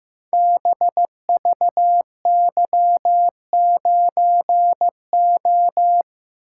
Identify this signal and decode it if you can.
Morse code, heard as a keyed tone, about 15 wpm, 700 Hz. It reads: BVY9O
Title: Code Groups - 5 Letter:Numbers - 15wpm